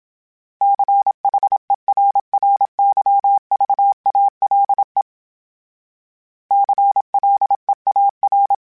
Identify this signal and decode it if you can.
Morse code, heard as a keyed tone, about 27 wpm, 800 Hz. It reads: CHERRYVALE CLEAR
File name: fromSMs / FromSMs mp3s / Cities@25